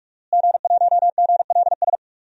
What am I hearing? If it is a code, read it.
Morse code, heard as a keyed tone, about 42 wpm, 700 Hz. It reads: G1GLS
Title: DXCC-100 Call Signs - 5 Characters - Repeated 2x - 45wpm